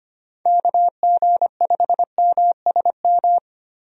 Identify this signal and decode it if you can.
Morse code, heard as a keyed tone, about 25 wpm, 700 Hz. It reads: KG5MSM
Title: USA Call Signs - 6 Characters - 25wpm